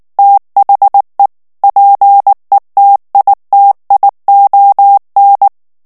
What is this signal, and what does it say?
Morse code, heard as a keyed tone, about 19 wpm, 800 Hz. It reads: THE PETITION
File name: ANGLAIS texte clair 3 vitesse 18 mots minute